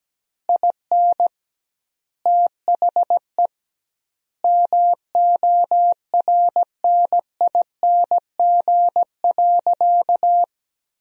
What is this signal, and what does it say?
Morse code, heard as a keyed tone, about 17 wpm, 700 Hz. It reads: IN THE MORNING.
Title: 2x Word-Spacing - Sentences from Top 400 Words - 17wpm